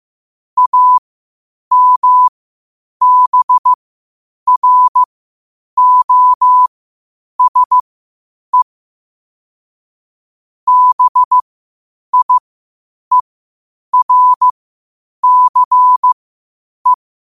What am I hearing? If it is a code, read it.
Morse code, heard as a keyed tone, about 15 wpm, spaced as for 8 wpm, 1000 Hz. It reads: AMBROSE BIERCE
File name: qotdfiles / 8wpm